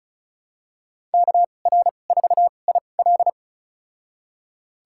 Morse code, heard as a keyed tone, about 36 wpm, spaced as for 25 wpm, 700 Hz. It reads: KR4IL
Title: USA Call Signs - 5 Characters - 2x Character Spacing - 35wpm